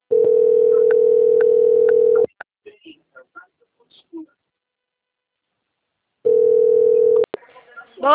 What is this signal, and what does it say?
Ringback tone (North America), 440 + 480 Hz, with speech.
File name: out- - - - - .60658